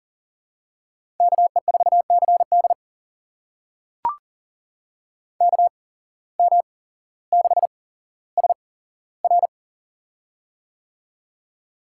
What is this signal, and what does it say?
Morse code, heard as a keyed tone, about 40 wpm, spaced as for 17 wpm, 700 Hz. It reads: KE4CD KM6SR
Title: USA Call Signs - 5 Characters - 8x Character Spacing - 40wpm